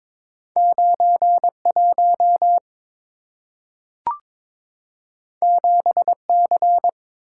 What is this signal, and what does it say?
Morse code, heard as a keyed tone, about 22 wpm, 690 Hz. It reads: 91 7C
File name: practice / 22wpm / Two Letter-Number Combinations - 22wpm